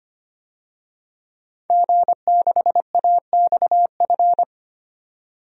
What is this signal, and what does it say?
Morse code, heard as a keyed tone, about 25 wpm, 700 Hz. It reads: G6AXF
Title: DXCC-100 Call Signs - 5 Characters - 25wpm